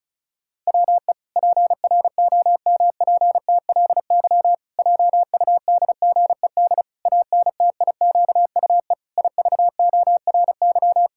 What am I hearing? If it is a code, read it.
Morse code, heard as a keyed tone, about 36 wpm, 700 Hz. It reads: WE PROMPTLY JUDGED ANTIQUE IVORY
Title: Pangram Challenge v2 - 35wpm